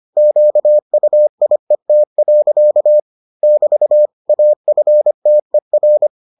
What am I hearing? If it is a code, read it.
Morse code, heard as a keyed tone, about 25 wpm, 600 Hz. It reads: QUIET. <BT> AFTER